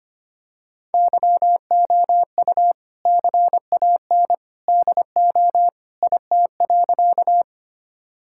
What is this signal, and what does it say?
Morse code, heard as a keyed tone, about 25 wpm, 700 Hz. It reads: YOU CAN DO IT.
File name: practice / 25wpm / Sentences with 2-Word Phrases - 25wpm